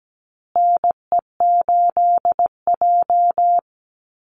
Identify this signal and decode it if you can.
Morse code, heard as a keyed tone, about 17 wpm, 700 Hz. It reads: NE8J
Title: Call Signs Repeated 3x - 17wpm